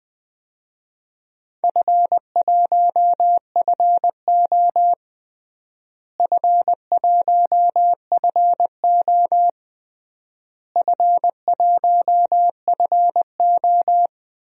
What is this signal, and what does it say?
Morse code, heard as a keyed tone, about 21 wpm, 700 Hz. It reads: F1FO F1FO F1FO
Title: DXCC-100 Call Signs - 4 Characters - Repeated 3x - 20wpm